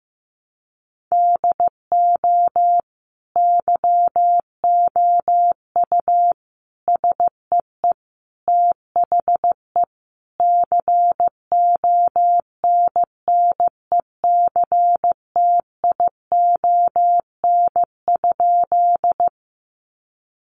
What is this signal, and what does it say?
Morse code, heard as a keyed tone, about 15 wpm, 700 Hz. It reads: DO YOU SEE THE CONNECTION?
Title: Sentences from Top 1300 Words 15wpm